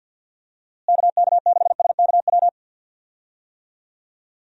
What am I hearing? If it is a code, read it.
Morse code, heard as a keyed tone, about 50 wpm, 700 Hz. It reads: KK6SKW
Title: USA Call Signs - 6 Characters - Repeated 3x - 50wpm